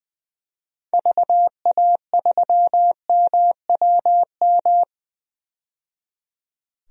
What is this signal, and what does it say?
Morse code, heard as a keyed tone, about 20 wpm, 700 Hz. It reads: VA3MWM
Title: DXCC-100 Call Signs - 6 Characters - 20wpm